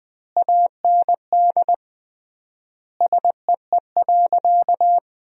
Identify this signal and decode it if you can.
Morse code, heard as a keyed tone, about 20 wpm, 700 Hz. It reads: AND SEE.
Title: 3x Word-Spacing - Sentences from Top 100 Words - 20wpm